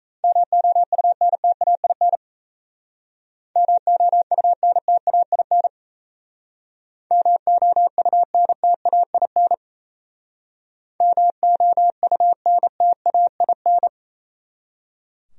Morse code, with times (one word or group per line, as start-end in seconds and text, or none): 0.2-2.2 MOUNTAIN
3.6-5.7 MOUNTAIN
7.1-9.5 MOUNTAIN
11.0-13.9 MOUNTAIN